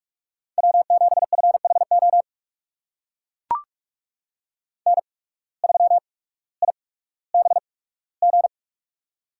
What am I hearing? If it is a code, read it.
Morse code, heard as a keyed tone, about 44 wpm, spaced as for 19 wpm, 690 Hz. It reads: W7FHO N3IBG
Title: USA Call Signs - 5 Characters - 8x Character Spacing - 45wpm